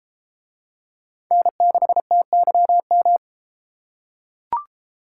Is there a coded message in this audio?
Yes: Morse code from 1.3 to 3.2 s, heard as a keyed tone, about 33 wpm, 700 Hz. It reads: N6TYM